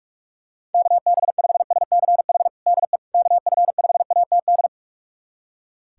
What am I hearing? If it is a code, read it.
Morse code, heard as a keyed tone, about 46 wpm, 700 Hz. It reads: KB5SCH DE KF5ATD
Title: USA Callsign DE Callsign - 6 Characters - 45wpm